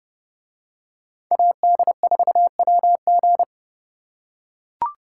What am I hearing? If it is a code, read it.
Morse code, heard as a keyed tone, about 28 wpm, 700 Hz. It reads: AD4WG